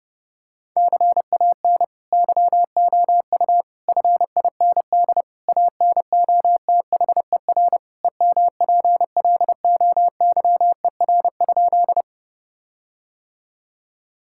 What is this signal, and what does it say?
Morse code, heard as a keyed tone, about 30 wpm, 700 Hz. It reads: CAN YOU FIND ANOTHER EMPLOYER?
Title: Sentences from Top 1600 Words 30wpm